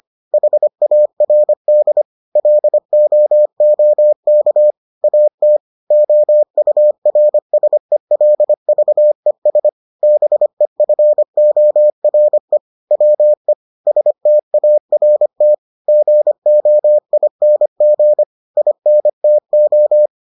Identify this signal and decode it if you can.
Morse code, heard as a keyed tone, about 25 wpm, 600 Hz. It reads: HARD LOOK AT OURSELVES BEFORE WE START GOING INTO